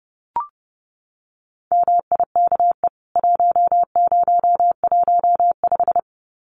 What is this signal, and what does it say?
Morse code, heard as a keyed tone, about 28 wpm, 700 Hz. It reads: MIKE 1015
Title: CWT Exchange 30wpm